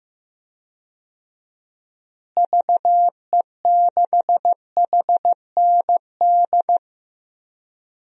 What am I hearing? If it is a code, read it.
Morse code, heard as a keyed tone, about 16 wpm, 700 Hz. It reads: VE6HND
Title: DXCC-100 Call Signs - 6 Characters - Repeated 2x - 15wpm